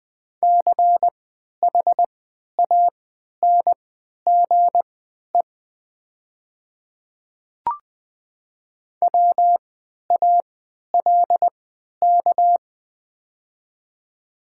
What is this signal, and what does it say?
Morse code, heard as a keyed tone, about 20 wpm, spaced as for 11 wpm, 700 Hz. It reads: CHANGE WALK
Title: 3x Character Spacing - Top 500 Words - 20wpm